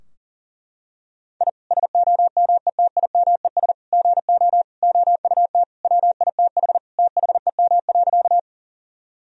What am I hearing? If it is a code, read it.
Morse code, heard as a keyed tone, about 38 wpm, 700 Hz. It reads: I SOMETIMES GO OUT WITH THEM.
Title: Sentences from Top 1000 Words - Review Part 3 - 40wpm